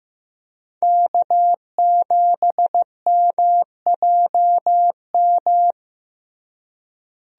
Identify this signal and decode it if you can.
Morse code, heard as a keyed tone, about 15 wpm, 700 Hz. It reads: K7MJM
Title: USA Call Signs - 5 Characters - Repeated 3x - 15wpm